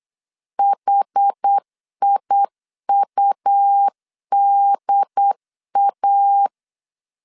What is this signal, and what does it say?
Morse code, heard as a keyed tone, about 8 wpm, 800 Hz. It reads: HIUDA
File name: CW / DET / 1 3 1500C S B 8wpm 4141